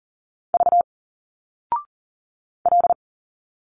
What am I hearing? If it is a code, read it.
Morse code, heard as a keyed tone, about 40 wpm, 700 Hz. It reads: V L
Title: Single Letters 40wpm